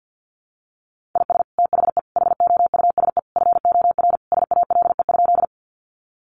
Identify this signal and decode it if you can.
Morse code, heard as a keyed tone, about 50 wpm, 700 Hz. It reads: IS THE HOUSE FOR SALE?
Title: Sentences from Top 1400 Words 50wpm